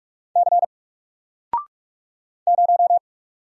Morse code, heard as a keyed tone, about 44 wpm, 700 Hz. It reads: C 0
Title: Single Letter-Number 45wpm